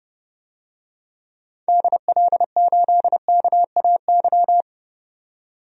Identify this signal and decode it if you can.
Morse code, heard as a keyed tone, about 31 wpm, 700 Hz. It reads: DL8KAY
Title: DXCC-100 Call Signs - 6 Characters - 30wpm